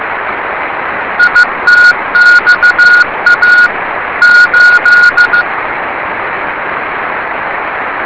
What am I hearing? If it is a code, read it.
Morse code, heard as a keyed tone, about 15 wpm, 1440 Hz, with radio noise. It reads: ITXA 8
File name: rs30 cw